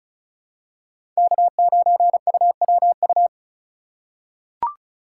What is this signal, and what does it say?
Morse code, heard as a keyed tone, about 34 wpm, 700 Hz. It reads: K9UWU
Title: USA Call Signs - 5 Characters - 4x Character Spacing - 35wpm